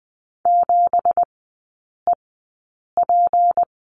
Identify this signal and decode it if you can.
Morse code, heard as a keyed tone, about 20 wpm, spaced as for 8 wpm, 700 Hz. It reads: 7EP
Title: 184 Lesson 30e - Letter b - Three Characters 20wpm